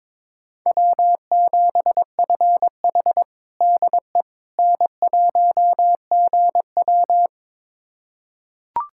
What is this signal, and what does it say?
Morse code, heard as a keyed tone, about 22 wpm, 700 Hz. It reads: W7FH DE N1GW
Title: USA Callsign DE Callsign - 4 Characters - 22wpm